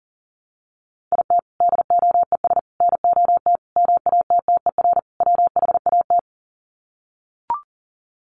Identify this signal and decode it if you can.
Morse code, heard as a keyed tone, about 40 wpm, 700 Hz. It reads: IT DOES NOT MATTER WHAT
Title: Sets of 5 Words Encore 40wpm